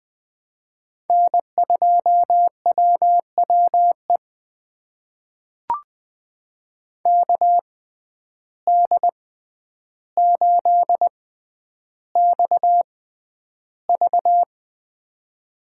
Morse code, heard as a keyed tone, about 20 wpm, spaced as for 10 wpm, 700 Hz. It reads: N2WWE KD8XV
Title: USA Call Signs - 5 Characters - 6x Character Spacing - 20wpm